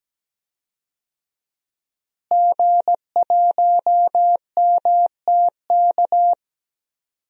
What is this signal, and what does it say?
Morse code, heard as a keyed tone, about 17 wpm, 700 Hz. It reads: G1MTK
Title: DXCC-100 Call Signs - 5 Characters - Repeated 2x - 17wpm